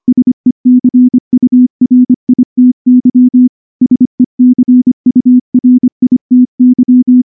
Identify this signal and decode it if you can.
Morse code, heard as a keyed tone, about 23 wpm, 250 Hz. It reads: SECURITY SECURITY
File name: morse / morse-training-25WPM